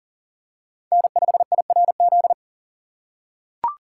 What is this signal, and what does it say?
Morse code, heard as a keyed tone, about 37 wpm, 700 Hz. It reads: N5IRZ
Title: USA Call Signs - 5 Characters - 4x Character Spacing - 40wpm